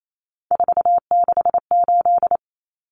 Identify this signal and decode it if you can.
Morse code, heard as a keyed tone, about 28 wpm, 700 Hz. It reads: 468
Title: Numbers - 3-Digit 28wpm